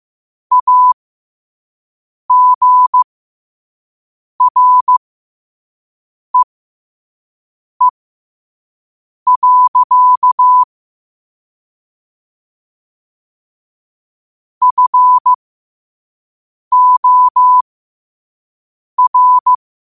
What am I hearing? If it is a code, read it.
Morse code, heard as a keyed tone, about 15 wpm, spaced as for 5 wpm, 1000 Hz. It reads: AGREE. FOR
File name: qotdfiles / 5wpm